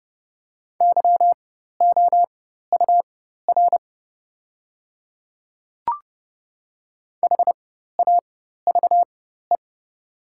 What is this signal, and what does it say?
Morse code, heard as a keyed tone, about 30 wpm, spaced as for 14 wpm, 700 Hz. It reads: YOUR HAVE
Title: 4x Character Spacing - Top 100 Words - 30wpm